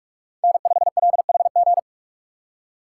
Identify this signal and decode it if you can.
Morse code, heard as a keyed tone, about 45 wpm, 700 Hz. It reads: N5LHG
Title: USA Call Signs - 3x Character Spacing - 45wpm